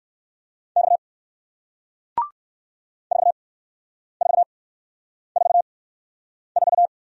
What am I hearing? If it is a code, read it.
Morse code, heard as a keyed tone, about 59 wpm, spaced as for 9 wpm, 700 Hz. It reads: X 4444